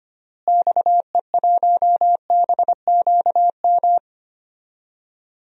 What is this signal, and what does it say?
Morse code, heard as a keyed tone, about 25 wpm, 700 Hz. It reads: XE1BQM
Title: DXCC-100 Call Signs - 6 Characters - Repeated 2x - 25wpm